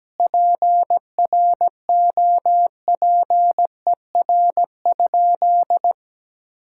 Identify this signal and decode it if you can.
Morse code, heard as a keyed tone, about 17 wpm, 700 Hz. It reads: PROPER?
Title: Sentences from Top 1000 Words - Review Part 5 - 17wpm